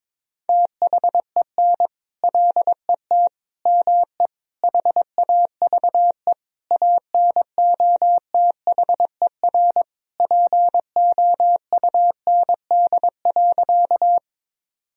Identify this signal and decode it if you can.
Morse code, heard as a keyed tone, about 22 wpm, 700 Hz. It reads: THEN LET ME HAVE ANOTHER POUND.
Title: Sentences from Top 1000 Words - Review Part 2 - 22wpm